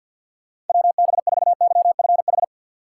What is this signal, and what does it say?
Morse code, heard as a keyed tone, about 52 wpm, 700 Hz. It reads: WB4YFH
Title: USA Call Signs - 6 Characters - 3x Character Spacing - 50wpm